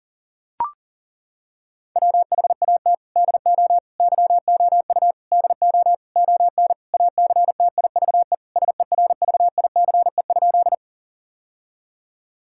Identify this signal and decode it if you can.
Morse code, heard as a keyed tone, about 40 wpm, 700 Hz. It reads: WHAT DO YOU DO ON ACTIVE SERVICE?
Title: Sentences from Top 1000 Words - Review Part 6 - 40wpm